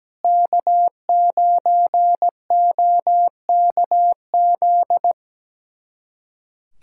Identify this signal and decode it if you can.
Morse code, heard as a keyed tone, about 17 wpm, 700 Hz. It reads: K9OKZ
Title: USA Callsign DE Callsign - 5 Characters - 17wpm